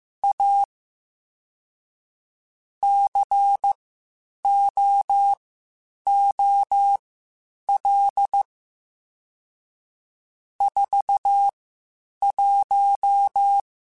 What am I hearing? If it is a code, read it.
Morse code, heard as a keyed tone, about 15 wpm, spaced as for 8 wpm, 800 Hz. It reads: A COOL 41
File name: Spaced Characters / Spaced Characters-mp3s / Weather 4 14 sp @13